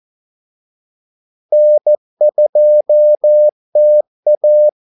Morse code, heard as a keyed tone, about 14 wpm, 600 Hz. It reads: N2TA
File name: Callsigns-14wpm